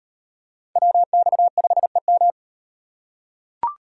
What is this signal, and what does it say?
Morse code, heard as a keyed tone, about 38 wpm, 700 Hz. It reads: WX5EM